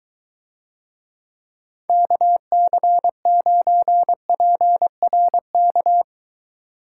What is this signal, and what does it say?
Morse code, heard as a keyed tone, about 23 wpm, 700 Hz. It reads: KC9PRK